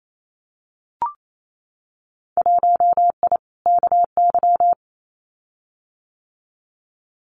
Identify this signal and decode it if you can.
Morse code, heard as a keyed tone, about 28 wpm, 700 Hz. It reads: 1I KY
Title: Winter Field Day Exchanges 28wpm